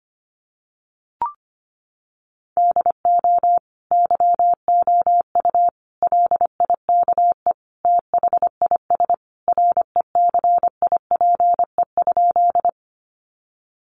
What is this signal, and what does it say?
Morse code, heard as a keyed tone, about 25 wpm, 700 Hz. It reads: DO YOU LIKE THIS RECIPE?
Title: Sentences from Top 1200 Words 25wpm